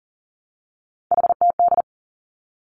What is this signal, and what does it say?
Morse code, heard as a keyed tone, about 42 wpm, 690 Hz. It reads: HTD
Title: Three Challenging Letter-Number Combinations 40wpm